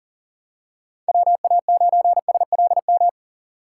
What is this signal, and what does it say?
Morse code, heard as a keyed tone, about 40 wpm, 700 Hz. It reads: WA9SLM
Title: USA Call Signs - 6 Characters - 6x Character Spacing - 40wpm